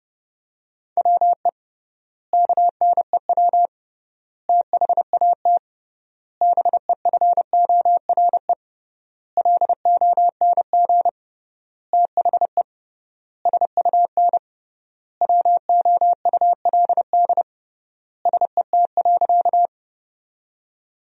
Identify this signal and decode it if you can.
Morse code, heard as a keyed tone, about 30 wpm, 700 Hz. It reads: WE KNEW THAT BEFORE LONG THE SUN WOULD SET.
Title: 3x Word-Spacing - Sentences from Top 400 Words - 30wpm